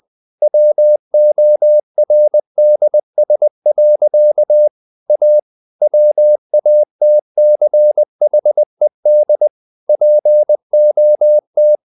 Morse code, heard as a keyed tone, about 20 wpm, 600 Hz. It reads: WORDS. A WATCHED POT
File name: CW / Practice / EnglishSayings20